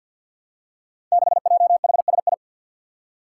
Morse code, heard as a keyed tone, about 46 wpm, 700 Hz. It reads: 6JHSI
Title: Five Challenging Letter-Number Combinations 50wpm